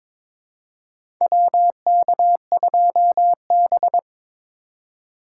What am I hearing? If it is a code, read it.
Morse code, heard as a keyed tone, about 22 wpm, 700 Hz. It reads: WK2B
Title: USA Call Signs - 8x Character Spacing - 22wpm